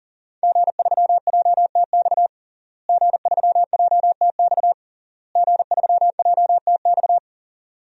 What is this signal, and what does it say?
Morse code, heard as a keyed tone, about 40 wpm, 700 Hz. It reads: G3JTX G3JTX G3JTX
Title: DXCC-100 Call Signs - 5 Characters - Repeated 3x - 40wpm